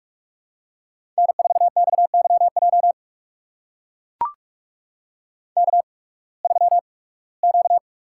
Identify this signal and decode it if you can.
Morse code, heard as a keyed tone, about 42 wpm, spaced as for 23 wpm, 700 Hz. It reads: N4XYJ K3Q
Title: USA Call Signs - 5 Characters - 8x Character Spacing - 45wpm